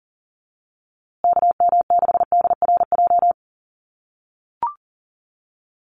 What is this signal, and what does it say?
Morse code, heard as a keyed tone, about 40 wpm, 700 Hz. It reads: KM6DRJ
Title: USA Call Signs Repeated 3x 40wpm